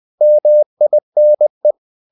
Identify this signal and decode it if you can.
Morse code, heard as a keyed tone, about 20 wpm, 600 Hz. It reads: MINE
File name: the old man and the sea / the old man and the sea part 1 20wpm